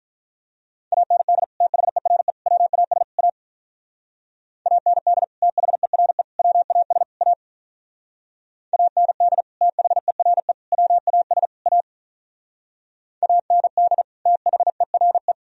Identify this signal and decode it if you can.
Morse code, heard as a keyed tone, about 46 wpm, 700 Hz. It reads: AND THERE WAS A AND THERE WAS A AND THERE WAS A AND THERE